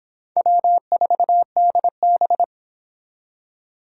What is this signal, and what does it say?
Morse code, heard as a keyed tone, about 30 wpm, 700 Hz. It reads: W4DB